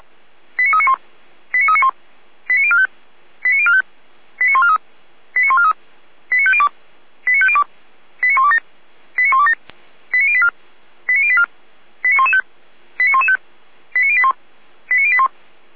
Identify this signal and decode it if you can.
Five-tone selective call (ZVEI), heard as quick runs of five tones, tones of about 80 ms. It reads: 89391 89391 89045 89045 89143 89143 89502 89502 89127 89127 89084 89084 89115 89115 89081 89081
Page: /455/SDR/ZVEI/